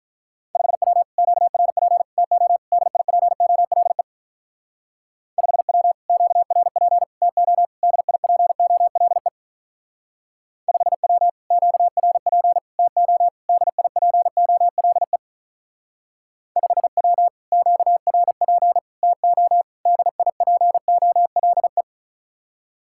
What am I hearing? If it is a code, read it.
Morse code, heard as a keyed tone, about 43 wpm, 700 Hz. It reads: 5W QRP TO DIPOLE 5W QRP TO DIPOLE 5W QRP TO DIPOLE 5W QRP TO DIPOLE
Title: Speed-Racing Common QSO Phrases Encore - 35wpm